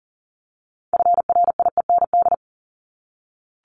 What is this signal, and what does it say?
Morse code, heard as a keyed tone, about 40 wpm, 700 Hz. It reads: FRIEND
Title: Top 500 Words 40wpm